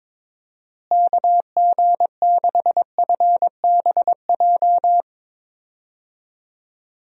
Morse code, heard as a keyed tone, about 22 wpm, 700 Hz. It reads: KG6FBJ
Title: USA Call Signs - 6 Characters - Repeated 2x - 22wpm